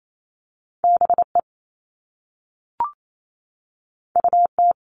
Morse code, heard as a keyed tone, about 28 wpm, 700 Hz. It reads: BE UT
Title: All Cut Numbers - 2-Digit 28wpm